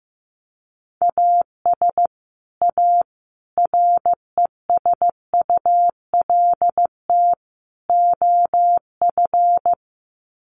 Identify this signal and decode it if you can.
Morse code, heard as a keyed tone, about 15 wpm, 700 Hz. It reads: AS A RESULT OF
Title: Sets of 4 Words 15wpm